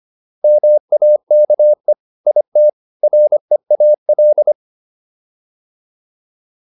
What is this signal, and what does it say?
Morse code, heard as a keyed tone, about 25 wpm, 600 Hz. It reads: MAKE IT REAL